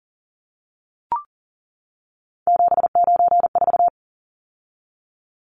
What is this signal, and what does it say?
Morse code, heard as a keyed tone, about 40 wpm, 700 Hz. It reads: 794
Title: Numbers - 3-Digit 40wpm